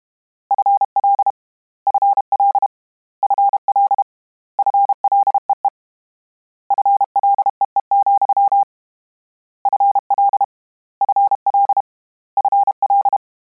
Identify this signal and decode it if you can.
Morse code, heard as a keyed tone, about 32 wpm, 800 Hz. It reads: FL FL FL FLEE FLEE, FL FL FL
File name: Combinations / Combinations-mp3s / Combinations@30